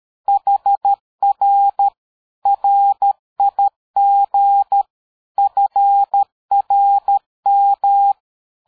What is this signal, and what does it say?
Morse code, heard as a keyed tone, about 13 wpm, 800 Hz. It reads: HR RIG FRM
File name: dk5ke / QSO 60BpM